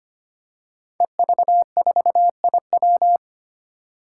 Morse code, heard as a keyed tone, about 25 wpm, 700 Hz. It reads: EV4IW
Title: Five Challenging Letter-Number Combinations 25wpm